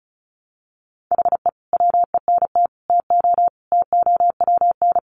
Morse code, heard as a keyed tone, about 34 wpm, 700 Hz. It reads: HE WENT TO TOWN
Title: Sentences from Top 1300 Words 35wpm